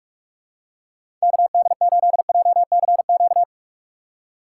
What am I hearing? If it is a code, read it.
Morse code, heard as a keyed tone, about 45 wpm, 700 Hz. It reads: KD8JCQ
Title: USA Call Signs - 6 Characters - 8x Character Spacing - 45wpm